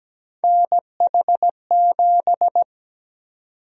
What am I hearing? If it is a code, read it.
Morse code, heard as a keyed tone, about 17 wpm, 700 Hz. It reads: NH7
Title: USA Call Sign Prefixes - Repeated 2x - 17wpm